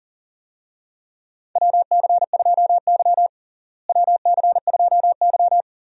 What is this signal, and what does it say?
Morse code, heard as a keyed tone, about 40 wpm, 700 Hz. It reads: WC2Y WC2Y
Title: USA Call Signs - 4 Characters - Repeated 2x - 40wpm